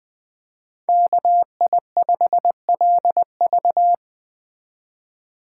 Morse code, heard as a keyed tone, about 21 wpm, 700 Hz. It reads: KI5LV